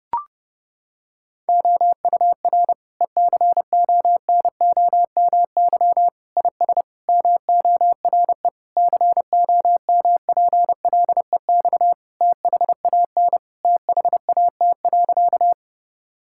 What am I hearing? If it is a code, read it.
Morse code, heard as a keyed tone, about 30 wpm, 700 Hz. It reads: OUR ECONOMY IS MORE COMPLEX THAN THAT.
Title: Sentences from Top 1000 Words - Review Part 6 - 30wpm